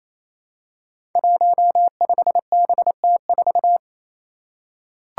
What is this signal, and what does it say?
Morse code, heard as a keyed tone, about 28 wpm, 700 Hz. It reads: 15BT4
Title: Five Challenging Letter-Number Combinations 28wpm